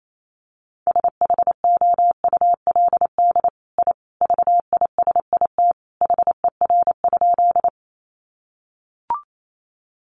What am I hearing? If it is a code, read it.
Morse code, heard as a keyed tone, about 28 wpm, 700 Hz. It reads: SHOULD I VISIT HER?
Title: Sentences from Top 900 Words 28wpm